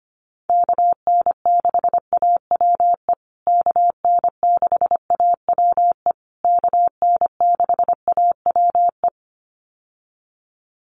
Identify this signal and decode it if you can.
Morse code, heard as a keyed tone, about 25 wpm, 700 Hz. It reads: KN6AWE KN6AWE KN6AWE
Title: USA Call Signs Repeated 3x 25wpm